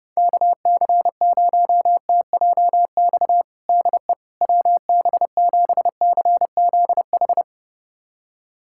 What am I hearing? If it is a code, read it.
Morse code, heard as a keyed tone, about 30 wpm, 700 Hz. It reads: KC0TJX DE WB7CZH
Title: USA Callsign DE Callsign 30wpm